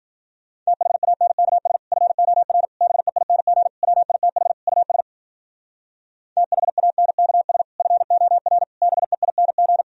Morse code, heard as a keyed tone, about 51 wpm, 700 Hz. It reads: THANKS FOR BEING WITH US THANKS FOR BEING